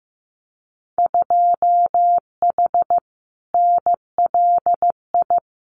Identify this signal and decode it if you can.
Morse code, heard as a keyed tone, about 15 wpm, 700 Hz. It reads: 2H NLI
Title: Winter Field Day Exchanges 15wpm